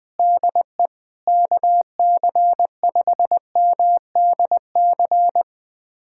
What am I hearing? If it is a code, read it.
Morse code, heard as a keyed tone, about 20 wpm, 700 Hz. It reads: DE KC5MDC
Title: USA Callsign DE Callsign - 6 Characters - 20wpm